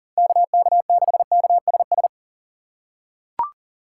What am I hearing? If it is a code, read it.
Morse code, heard as a keyed tone, about 40 wpm, 700 Hz. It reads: KK6KSS
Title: USA Call Signs - 6 Characters - 4x Character Spacing - 40wpm